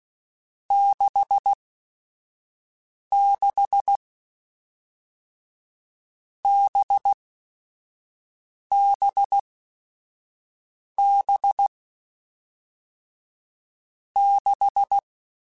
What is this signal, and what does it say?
Morse code, heard as a keyed tone, about 16 wpm, spaced as for 5 wpm, 800 Hz. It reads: 66 BBB 6
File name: Groups@5 / Groups@5 mp3s / VB46@5